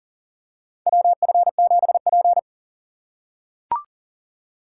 Morse code, heard as a keyed tone, about 40 wpm, 700 Hz. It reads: WF7P
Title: USA Call Signs - 4 Characters - 4x Character Spacing - 40wpm